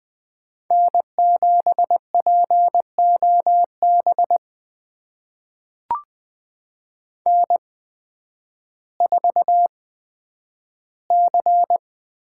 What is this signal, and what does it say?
Morse code, heard as a keyed tone, about 20 wpm, spaced as for 11 wpm, 700 Hz. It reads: N7POB N4C
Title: USA Call Signs - 8x Character Spacing - 20wpm